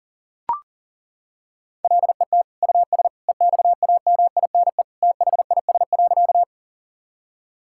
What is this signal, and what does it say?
Morse code, heard as a keyed tone, about 40 wpm, 700 Hz. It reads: LET US EXAMINE THIS.
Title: Sentences from Top 1600 Words 40wpm